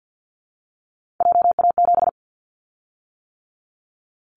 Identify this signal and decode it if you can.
Morse code, heard as a keyed tone, about 50 wpm, 700 Hz. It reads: JA7
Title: Callsign Prefixes for Top 25 DXCC - 50wpm